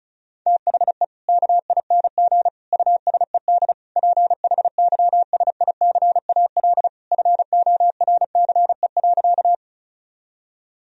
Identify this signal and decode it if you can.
Morse code, heard as a keyed tone, about 34 wpm, 700 Hz. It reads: THE KING USED PHYSICAL FORCE.
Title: Sentences from Top 1000 Words - Review Part 4 - 35wpm